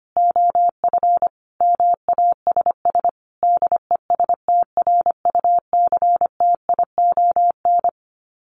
Morse code, heard as a keyed tone, about 25 wpm, 700 Hz. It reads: OF MASS DESTRUCTION
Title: Sets of 5 Words Encore 25wpm